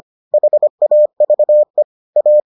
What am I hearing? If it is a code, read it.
Morse code, heard as a keyed tone, about 25 wpm, 600 Hz. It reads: HAVE A